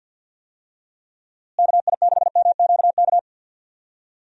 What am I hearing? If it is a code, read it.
Morse code, heard as a keyed tone, about 52 wpm, 700 Hz. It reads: KI6MQK
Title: USA Call Signs - 6 Characters - Repeated 3x - 50wpm